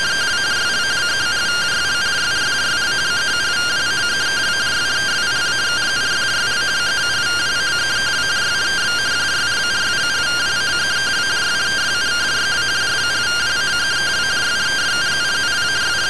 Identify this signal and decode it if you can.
Two-tone teleprinter signal, about 170 Hz shift.